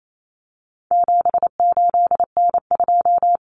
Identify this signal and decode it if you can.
Morse code, heard as a keyed tone, about 28 wpm, 700 Hz. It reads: 78N2
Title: Common Cut Numbers - 4-Digit 28wpm